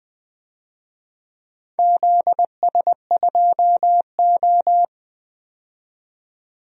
Morse code, heard as a keyed tone, about 20 wpm, 700 Hz. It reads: ZS2O
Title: DXCC-100 Call Signs - 4 Characters - 20wpm